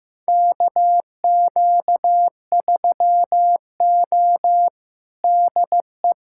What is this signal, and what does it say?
Morse code, heard as a keyed tone, about 15 wpm, 700 Hz. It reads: KQ3O DE
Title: USA Callsign DE Callsign - 4 Characters - 15wpm